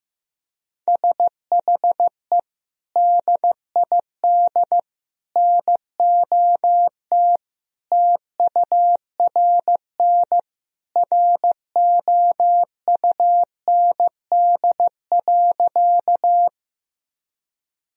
Morse code, heard as a keyed tone, about 15 wpm, 700 Hz. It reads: SHE DID NOT TURN ROUND.
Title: Sentences from Top 1000 Words - Review Part 1 - 15wpm